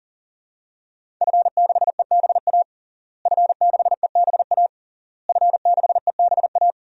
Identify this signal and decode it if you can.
Morse code, heard as a keyed tone, about 40 wpm, 700 Hz. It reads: F6EBA F6EBA F6EBA
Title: DXCC-100 Call Signs - 5 Characters - Repeated 3x - 40wpm